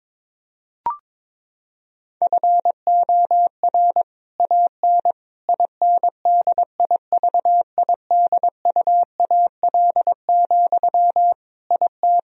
Morse code, heard as a keyed tone, about 22 wpm, 700 Hz. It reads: FOR AN INDIVIDUAL, IT
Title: Sentences from Top 1000 Words - Review Part 4 - 22wpm